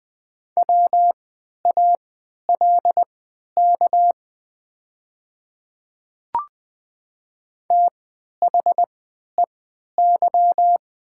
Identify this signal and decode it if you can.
Morse code, heard as a keyed tone, about 20 wpm, spaced as for 11 wpm, 700 Hz. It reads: WALK THEY
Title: 3x Character Spacing - Top 300 Words - 20wpm